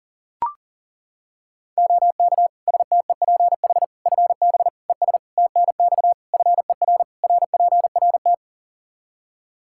Morse code, heard as a keyed tone, about 40 wpm, 700 Hz. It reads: OK STEPH FB ES TNX FER RPRT
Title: Practice QSO - Exchange 2 - Info - 40wpm